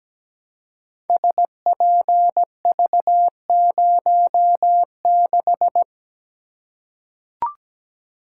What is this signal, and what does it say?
Morse code, heard as a keyed tone, about 17 wpm, 700 Hz. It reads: SPV06
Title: Code Groups - 5 Letter:Numbers - 17wpm